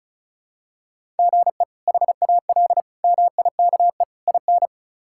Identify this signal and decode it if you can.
Morse code, heard as a keyed tone, about 34 wpm, 700 Hz. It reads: GE HAL MIKE IN
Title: K1USN SST QSO Practice - 35wpm